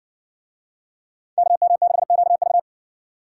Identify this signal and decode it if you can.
Morse code, heard as a keyed tone, about 58 wpm, 700 Hz. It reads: KM6QV